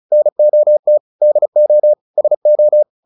Morse code, heard as a keyed tone, about 34 wpm, 600 Hz. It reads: NOT DO SO
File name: the old man and the sea 2 35wpm